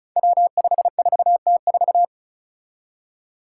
Morse code, heard as a keyed tone, about 36 wpm, 700 Hz. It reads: W54T4
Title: Five Challenging Letter-Number Combinations 35wpm